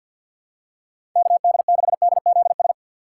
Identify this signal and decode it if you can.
Morse code, heard as a keyed tone, about 49 wpm, 700 Hz. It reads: KD6DZS
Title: USA Call Signs - 6 Characters - 6x Character Spacing - 50wpm